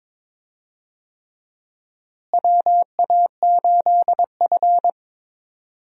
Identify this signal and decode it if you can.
Morse code, heard as a keyed tone, about 22 wpm, 700 Hz. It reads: WA8F
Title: USA Call Signs - 4 Characters - Repeated 3x - 22wpm